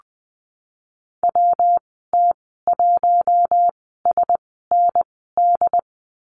Morse code, heard as a keyed tone, about 20 wpm, spaced as for 15 wpm, 700 Hz. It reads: WT1SND
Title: 167 Lesson 27g - Number 7 - Call Signs 20wpm